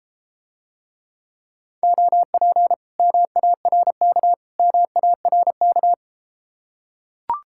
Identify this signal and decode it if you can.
Morse code, heard as a keyed tone, about 33 wpm, 700 Hz. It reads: OP MARK MARK